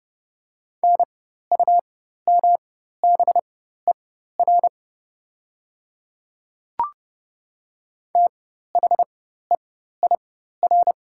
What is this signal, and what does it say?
Morse code, heard as a keyed tone, about 30 wpm, spaced as for 14 wpm, 700 Hz. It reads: NUMBER THEIR
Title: 4x Character Spacing - Top 100 Words - 30wpm